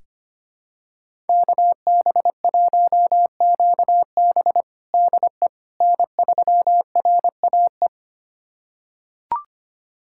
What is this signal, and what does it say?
Morse code, heard as a keyed tone, about 25 wpm, 700 Hz. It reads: KB1QB DE N3RAE
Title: USA Callsign DE Callsign - 5 Characters - 25wpm